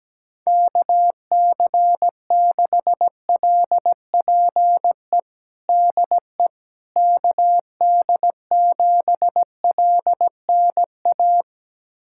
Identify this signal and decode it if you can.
Morse code, heard as a keyed tone, about 17 wpm, 700 Hz. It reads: KC6LPE DE KD7LNA
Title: USA Callsign DE Callsign - 6 Characters - 17wpm